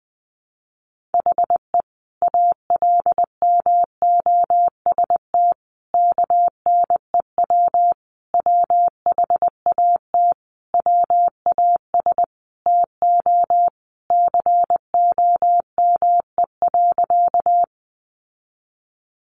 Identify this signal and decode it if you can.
Morse code, heard as a keyed tone, about 20 wpm, 700 Hz. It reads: HE ALMOST KNEW WHAT WAS TO COME.
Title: Sentences from Top 700 Words 20wpm